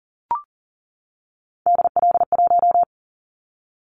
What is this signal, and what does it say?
Morse code, heard as a keyed tone, about 40 wpm, 700 Hz. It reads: DL1
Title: Callsign Prefixes for Top 50 DXCC - 40wpm